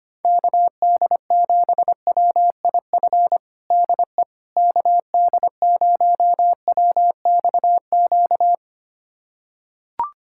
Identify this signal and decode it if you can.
Morse code, heard as a keyed tone, about 25 wpm, 700 Hz. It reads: KD7WIF DE KD0WXQ
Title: USA Callsign DE Callsign - 6 Characters - 25wpm